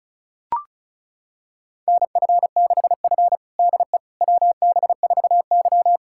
Morse code, heard as a keyed tone, about 36 wpm, 700 Hz. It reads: NF6F DE WB4Y
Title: USA Callsign DE Callsign - 4 Characters - 35wpm